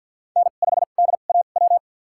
Morse code, heard as a keyed tone, about 51 wpm, spaced as for 36 wpm, 700 Hz. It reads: N5DAW
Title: USA Call Signs - 5 Characters - 2x Character Spacing - 50wpm